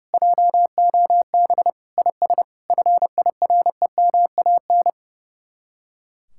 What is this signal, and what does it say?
Morse code, heard as a keyed tone, about 38 wpm, 700 Hz. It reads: JOB IS FIREMAN